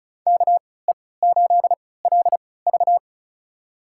Morse code, heard as a keyed tone, about 36 wpm, spaced as for 20 wpm, 700 Hz. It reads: KE8LV
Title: USA Call Signs - 3x Character Spacing - 35wpm